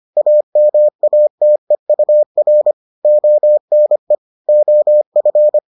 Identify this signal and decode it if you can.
Morse code, heard as a keyed tone, about 25 wpm, 600 Hz. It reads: AMATEUR ONE OF